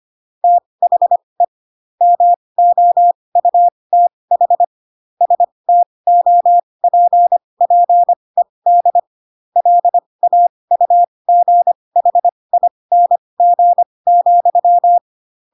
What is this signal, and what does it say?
Morse code, heard as a keyed tone, about 25 wpm, spaced as for 20 wpm, 700 Hz. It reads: THE MOUTH STOPPED LAUGHING,
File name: Pinocchio / Pinocchio ch3 15 25 ik1hge